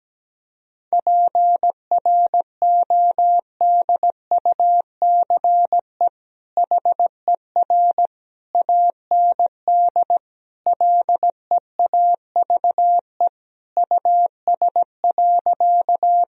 Morse code, heard as a keyed tone, about 17 wpm, 700 Hz. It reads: PRODUCE HER AND LEAVE US.
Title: Sentences from Top 1000 Words - Review Part 3 - 17wpm